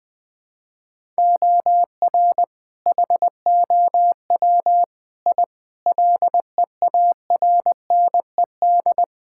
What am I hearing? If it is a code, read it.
Morse code, heard as a keyed tone, about 19 wpm, 700 Hz. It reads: OR HOW I LEARNED
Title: Sentences with 8-Word Phrases - 20wpm